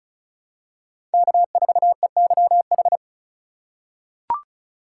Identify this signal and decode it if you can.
Morse code, heard as a keyed tone, about 36 wpm, 700 Hz. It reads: K4EYH
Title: USA Call Signs - 5 Characters - 3x Character Spacing - 35wpm